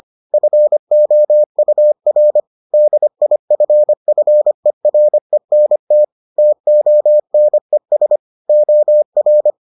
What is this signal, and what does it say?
Morse code, heard as a keyed tone, about 25 wpm, 600 Hz. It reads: FOUR DIFFERENT TONES OR